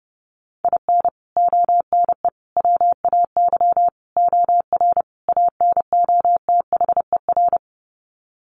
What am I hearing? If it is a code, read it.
Morse code, heard as a keyed tone, about 30 wpm, 700 Hz. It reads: IN ONE WAY OR ANOTHER
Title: Sets of 5 Words Encore 30wpm